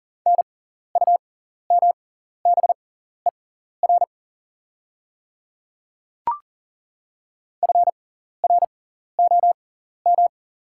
Morse code, heard as a keyed tone, about 40 wpm, spaced as for 14 wpm, 690 Hz. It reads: NUMBER FROM
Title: 6x Character Spacing - Top 200 Words - 40wpm